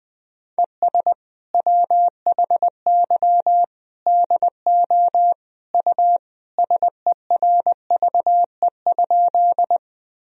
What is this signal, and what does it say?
Morse code, heard as a keyed tone, about 20 wpm, 700 Hz. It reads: ES WHY DO U SERVE?
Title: Sentences from Top 400 Words and Abbreviations 20wpm